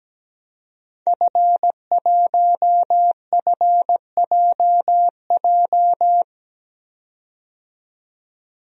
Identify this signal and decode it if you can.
Morse code, heard as a keyed tone, about 17 wpm, 700 Hz. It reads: F1FJJ
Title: DXCC-100 Call Signs - 5 Characters - 17wpm